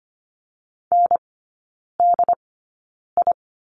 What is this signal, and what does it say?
Morse code, heard as a keyed tone, about 25 wpm, spaced as for 9 wpm, 700 Hz. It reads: NDI
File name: Lesson 14e - Number 2 - Three Characters 25wpm